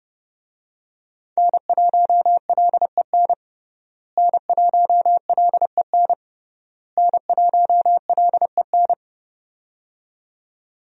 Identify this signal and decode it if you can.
Morse code, heard as a keyed tone, about 30 wpm, 700 Hz. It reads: N1LEN N1LEN N1LEN
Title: USA Call Signs - 5 Characters - Repeated 3x - 30wpm